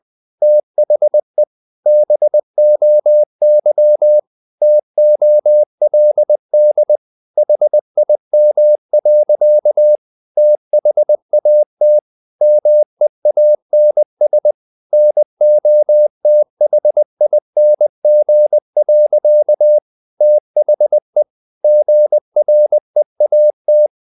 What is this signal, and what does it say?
Morse code, heard as a keyed tone, about 20 wpm, 600 Hz. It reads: THE BOY TOLD HIM. THAT MEANS NOTHING. THE GREAT
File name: the old man and the sea / the old man and the sea part 1 20wpm